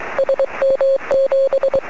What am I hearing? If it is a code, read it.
Morse code, heard as a keyed tone, about 23 wpm, 540 Hz. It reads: SM7
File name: sm7lcb / 06-yl3ag 23cm